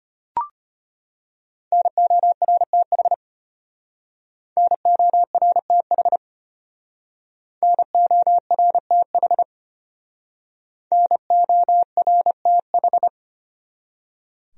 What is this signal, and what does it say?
Morse code, heard as a keyed tone, about 31 wpm, 700 Hz. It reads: NORTH NORTH NORTH NORTH
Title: Speed-Racing Top 500 Words 25wpm